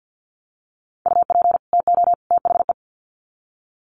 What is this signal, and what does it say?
Morse code, heard as a keyed tone, about 50 wpm, 700 Hz. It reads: UP TO THE
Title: Sets of 3 Words 50wpm